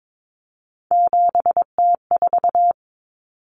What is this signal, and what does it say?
Morse code, heard as a keyed tone, about 22 wpm, 700 Hz. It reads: 7T4
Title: Common Cut Numbers - 3-Digit 22wpm